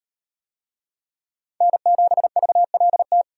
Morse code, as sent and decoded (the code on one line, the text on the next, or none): -. --... ...- .-.. -
N7VLT